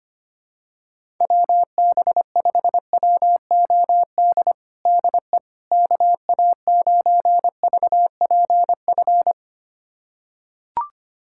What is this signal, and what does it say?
Morse code, heard as a keyed tone, about 25 wpm, 700 Hz. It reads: WB5WOD DE KA9VPF
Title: USA Callsign DE Callsign - 6 Characters - 25wpm